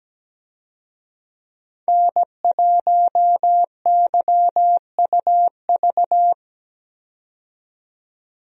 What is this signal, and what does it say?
Morse code, heard as a keyed tone, about 17 wpm, 700 Hz. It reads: N1YUV
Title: USA Call Signs - 5 Characters - 17wpm